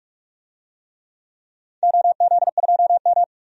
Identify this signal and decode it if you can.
Morse code, heard as a keyed tone, about 42 wpm, 700 Hz. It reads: OZ2M
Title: DXCC-100 Call Signs - 4 Characters - 45wpm